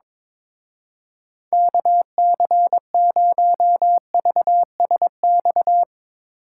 Morse code, heard as a keyed tone, about 22 wpm, 700 Hz. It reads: KC0VSX
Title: Speed Racing - USA 2x3 Call Signs 22wpm